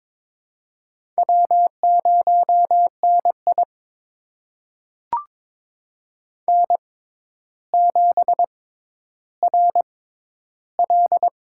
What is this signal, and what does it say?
Morse code, heard as a keyed tone, about 22 wpm, spaced as for 11 wpm, 700 Hz. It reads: W0NI N7RL
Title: USA Call Signs - 4 Characters - 6x Character Spacing - 22wpm